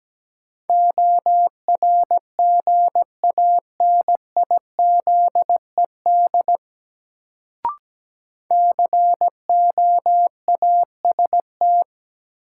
Morse code, heard as a keyed tone, about 17 wpm, 700 Hz. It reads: ORGANIZED COAST
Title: Sentences from Top 1300 Words - New Word Drill Alt - 17wpm